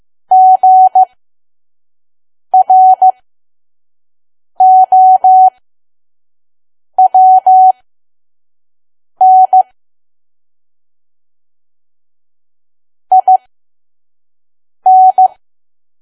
Morse code, heard as a keyed tone, about 15 wpm, spaced as for 5 wpm, 750 Hz. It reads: GROWN IN
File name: Archive / 5 WPM files / PM